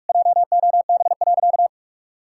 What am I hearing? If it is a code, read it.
Morse code, heard as a keyed tone, about 41 wpm, 700 Hz. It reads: JOB.
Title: Sentences from Top 1600 Words 45wpm